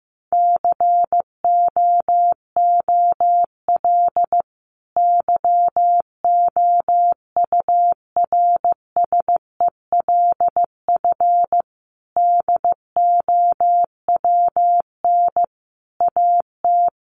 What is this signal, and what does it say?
Morse code, heard as a keyed tone, about 15 wpm, 700 Hz. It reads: COOL YOURSELF DOWN AT
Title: Sentences from Top 1200 Words 15wpm